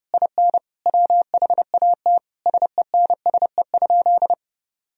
Morse code, heard as a keyed tone, about 30 wpm, 700 Hz. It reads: IN WHAT SENSE?
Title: Sentences from Top 1000 Words - Review Part 4 - 30wpm